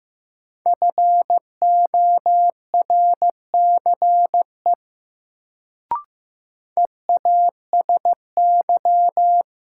Morse code, heard as a keyed tone, about 15 wpm, 700 Hz. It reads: FORCE EASY
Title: Sentences from Top 700 Words - New Word Drill - 15wpm